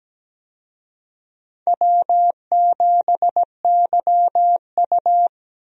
Morse code, heard as a keyed tone, about 17 wpm, 700 Hz. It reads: W7YU
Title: USA Call Signs - 4 Characters - Repeated 3x - 17wpm